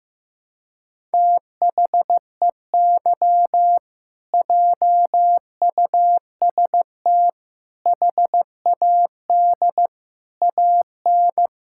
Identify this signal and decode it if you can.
Morse code, heard as a keyed tone, about 14 wpm, 700 Hz. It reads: THEY JUST HAD AN
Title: Sentences from Top 1000 Words - Review Part 6 - 15wpm